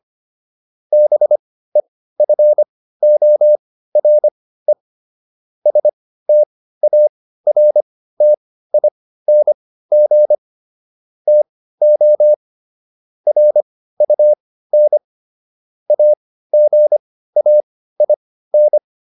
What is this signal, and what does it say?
Morse code, heard as a keyed tone, about 25 wpm, spaced as for 15 wpm, 600 Hz. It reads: BEFORE STARTING TO RUN AGAIN